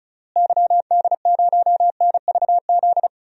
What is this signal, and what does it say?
Morse code, heard as a keyed tone, about 36 wpm, 700 Hz. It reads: YD0NVZ
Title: DXCC-100 Call Signs - 6 Characters - Repeated 3x - 35wpm